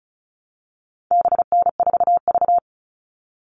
Morse code, heard as a keyed tone, about 34 wpm, 700 Hz. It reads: BN4V
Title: All Cut Numbers - 4-Digit 35wpm